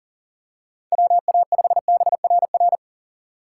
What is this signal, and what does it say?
Morse code, heard as a keyed tone, about 40 wpm, 700 Hz. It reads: WA5BRR
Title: USA Call Signs - 6 Characters - 6x Character Spacing - 40wpm